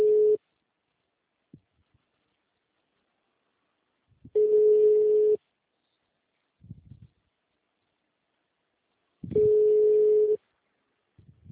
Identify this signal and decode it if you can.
Ringback tone (Europe, 425 Hz).
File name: out- -2011- - - .322